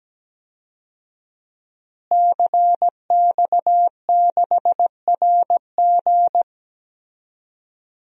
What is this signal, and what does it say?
Morse code, heard as a keyed tone, about 17 wpm, 700 Hz. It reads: CX6RG
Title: DXCC-100 Call Signs - 5 Characters - Repeated 2x - 17wpm